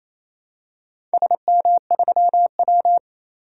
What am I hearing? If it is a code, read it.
Morse code, heard as a keyed tone, about 29 wpm, 700 Hz. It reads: SM3W
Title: DXCC-100 Call Signs - 4 Characters - 28wpm